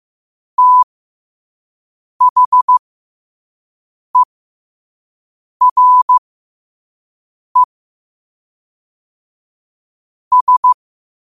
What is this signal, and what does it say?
Morse code, heard as a keyed tone, about 15 wpm, spaced as for 5 wpm, 1000 Hz. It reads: THERE S
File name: qotdfiles / 5wpm